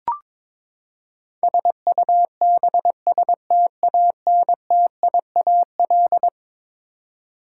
Morse code, heard as a keyed tone, about 22 wpm, 700 Hz. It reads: SUBSTANTIAL
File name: practice / 22wpm / Eleven Letter Words - 22wpm